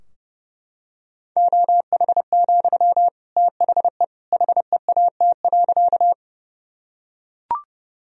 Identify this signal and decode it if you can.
Morse code, heard as a keyed tone, about 32 wpm, 700 Hz. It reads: OH, THE HEAT.
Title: Sentences from Top 1000 Words - Review Part 3 - 30wpm